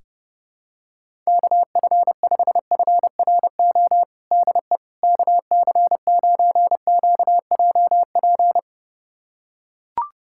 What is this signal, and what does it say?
Morse code, heard as a keyed tone, about 30 wpm, 700 Hz. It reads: KF5FRO DE KC9QJP
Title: USA Callsign DE Callsign - 6 Characters - 30wpm